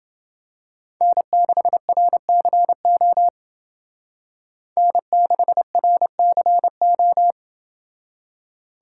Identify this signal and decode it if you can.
Morse code, heard as a keyed tone, about 28 wpm, 700 Hz. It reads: N6RCO N6RCO